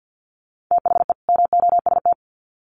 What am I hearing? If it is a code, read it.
Morse code, heard as a keyed tone, about 50 wpm, 700 Hz. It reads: THE MOST
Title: Sets of 2 Words 50wpm